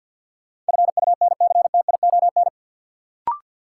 Morse code, heard as a keyed tone, about 49 wpm, 700 Hz. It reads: FUNCTION
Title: Sentences from Top 1100 Words - New Word Drill - 50wpm